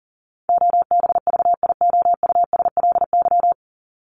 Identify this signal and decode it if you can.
Morse code, heard as a keyed tone, about 40 wpm, 700 Hz. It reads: OBVIOUSLY
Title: LY Suffix - Top 100 Words with Base Words 40wpm